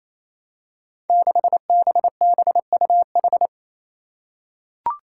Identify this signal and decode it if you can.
Morse code, heard as a keyed tone, about 28 wpm, 700 Hz. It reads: 6BBUH
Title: Five Challenging Letter-Number Combinations 28wpm